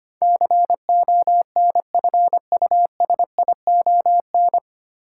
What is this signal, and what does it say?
Morse code, heard as a keyed tone, about 25 wpm, 700 Hz. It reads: CONFUSION
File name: Nine Letter Words - 25wpm